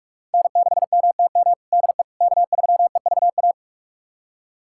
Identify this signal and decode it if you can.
Morse code, heard as a keyed tone, about 46 wpm, 700 Hz. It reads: N6MTM DE K3EVA
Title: USA Callsign DE Callsign - 5 Characters - 45wpm